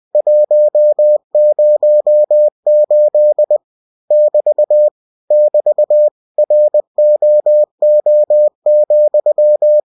Morse code, heard as a keyed tone, about 20 wpm, 600 Hz. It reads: 108 <BT> <BT> ROO,
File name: cw-books / winnie-the-pooh / winnie-20wpm-page-0108